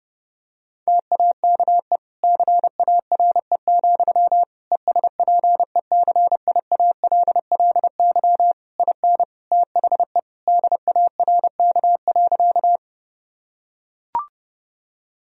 Morse code, heard as a keyed tone, about 30 wpm, 700 Hz. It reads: TAKE CARE, ESPECIALLY IN THE DARK.
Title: Sentences from Top 1000 Words - Review Part 4 - 30wpm